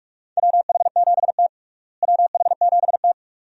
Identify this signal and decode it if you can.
Morse code, heard as a keyed tone, about 42 wpm, 700 Hz. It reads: WH7T WH7T
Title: USA Call Signs - 4 Characters - Repeated 2x - 45wpm